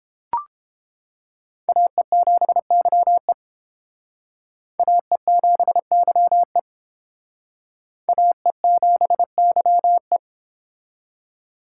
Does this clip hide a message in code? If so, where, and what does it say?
Yes: Morse code from 1.7 to 10.2 s, heard as a keyed tone, about 28 wpm, 700 Hz. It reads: AE7YE AE7YE AE7YE